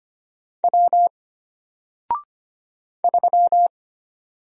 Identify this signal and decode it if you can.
Morse code, heard as a keyed tone, about 25 wpm, 700 Hz. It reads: W 3
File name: Single-Letter-Number-25wpm